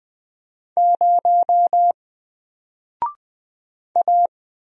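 Morse code, heard as a keyed tone, about 20 wpm, 700 Hz. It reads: 0 A
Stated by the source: Single letter and number Morse drill at 20 wpm, from N4LCD.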